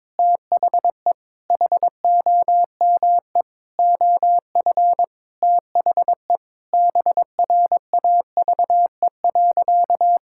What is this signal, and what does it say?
Morse code, heard as a keyed tone, about 22 wpm, 700 Hz. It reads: THE HOME OF THE BRAVE.
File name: practice / 22wpm / Sentences with 8-Word Phrases - 22wpm